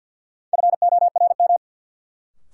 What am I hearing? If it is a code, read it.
Morse code, heard as a keyed tone, about 52 wpm, 700 Hz. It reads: FORM
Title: Sentences from Top 1000 Words - Review Part 1 - New Word Drill - 50wpm